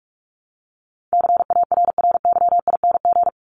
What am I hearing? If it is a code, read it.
Morse code, heard as a keyed tone, about 44 wpm, 700 Hz. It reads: CARRYING
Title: ING Suffix - Top 100 Words with Base Words 45wpm